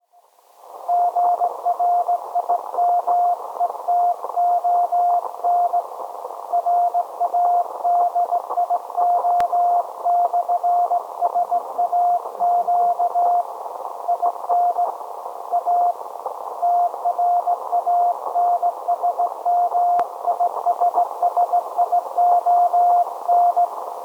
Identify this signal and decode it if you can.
Morse code, heard as a keyed tone, about 17 wpm, 700 Hz. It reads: GRIMETON RADIO/SAQ IN A TRANSMISSION
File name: storage / message Grimeton SAQ